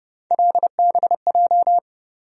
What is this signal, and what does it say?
Morse code, heard as a keyed tone, about 31 wpm, 700 Hz. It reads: LBJ
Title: USA Call Sign Postfixes - 3 Letters 30wpm